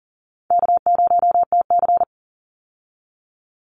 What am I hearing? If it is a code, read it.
Morse code, heard as a keyed tone, about 40 wpm, 700 Hz. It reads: K0TC
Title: Call Signs Repeated 3x - 40wpm